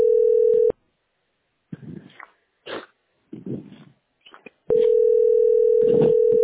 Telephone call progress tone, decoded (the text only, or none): ringback tone